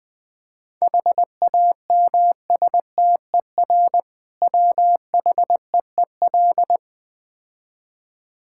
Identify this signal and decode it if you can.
Morse code, heard as a keyed tone, about 20 wpm, 690 Hz. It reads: HAMSTER WHEEL
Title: Difficult Words with Hints - 20wpm